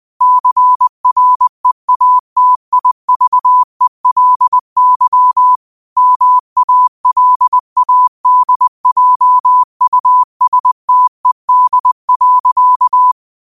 Morse code, heard as a keyed tone, about 20 wpm, 1000 Hz. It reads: CREATIVELY MALADJUSTED.
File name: qotdfiles / 20wpm